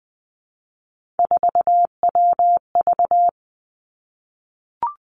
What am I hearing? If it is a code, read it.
Morse code, heard as a keyed tone, about 20 wpm, 700 Hz. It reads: 4WV
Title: Three Challenging Letter-Number Combinations 20wpm